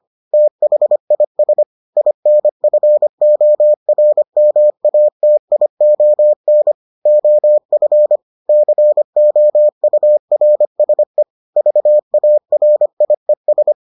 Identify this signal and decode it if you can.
Morse code, heard as a keyed tone, about 25 wpm, 600 Hz. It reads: THIS INFORMATION OF COURSE VARIES